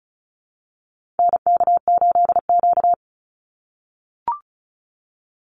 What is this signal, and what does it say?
Morse code, heard as a keyed tone, about 34 wpm, 700 Hz. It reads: NK8Q
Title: NA SOTA Activator Callsigns 35wpm